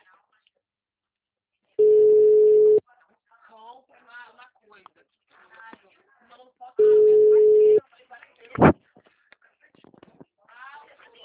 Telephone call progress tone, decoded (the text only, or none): ringback tone